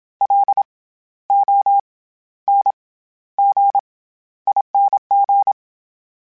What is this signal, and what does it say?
Morse code, heard as a keyed tone, about 27 wpm, 800 Hz. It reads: L O N G ING
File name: Special INGs-mp3s / Special INGs-mp3s / SlowToFaster ing @25